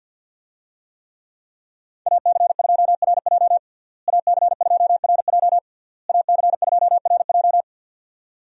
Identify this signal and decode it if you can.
Morse code, heard as a keyed tone, about 52 wpm, 700 Hz. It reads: AC2RJ AC2RJ AC2RJ
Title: USA Call Signs - 5 Characters - Repeated 3x - 50wpm